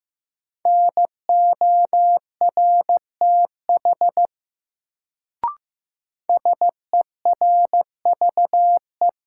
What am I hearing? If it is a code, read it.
Morse code, heard as a keyed tone, about 15 wpm, 700 Hz. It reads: NORTH SERVE
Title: Sentences from Top 400 Words - New Word Drill - 15wpm